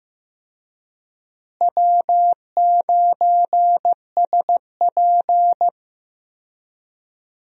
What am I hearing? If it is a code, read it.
Morse code, heard as a keyed tone, about 16 wpm, 700 Hz. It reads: W9SP
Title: USA Call Signs - 4 Characters - Repeated 3x - 15wpm